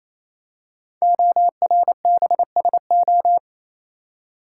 Morse code, heard as a keyed tone, about 28 wpm, 700 Hz. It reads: ORBSO
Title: Code Groups - 5 Letter:Numbers - 28wpm